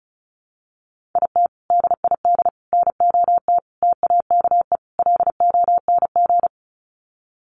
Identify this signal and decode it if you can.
Morse code, heard as a keyed tone, about 34 wpm, 700 Hz. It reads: IT DID NOT TAKE LONG
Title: Sets of 5 Words Encore2 35wpm